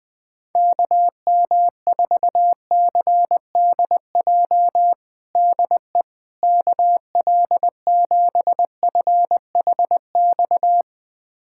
Morse code, heard as a keyed tone, about 21 wpm, 700 Hz. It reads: KM4CDJ DE KL7FHX
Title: USA Callsign DE Callsign - 6 Characters - 20wpm